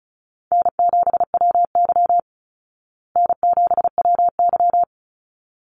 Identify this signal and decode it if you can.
Morse code, heard as a keyed tone, about 34 wpm, 700 Hz. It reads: N7WY N7WY
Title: Call Signs Repeated 3x - 35wpm